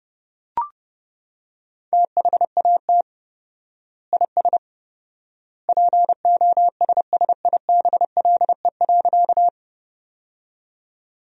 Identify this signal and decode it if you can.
Morse code, heard as a keyed tone, about 30 wpm, 700 Hz. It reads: THAT IS POSSIBLE.
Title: 4x Word-Spacing - Sentences from Top 500 Words - 30wpm